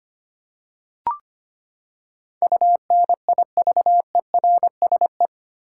Morse code, heard as a keyed tone, about 25 wpm, 700 Hz. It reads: UNIVERSE
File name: Eight Letter Words - 25wpm